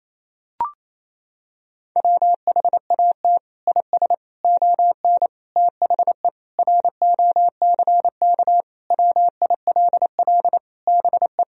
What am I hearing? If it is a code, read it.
Morse code, heard as a keyed tone, about 28 wpm, 700 Hz. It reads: WHAT IS ON THE ROCK WILL BE
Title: Sentences from Top 1000 Words - Review Part 2 - 28wpm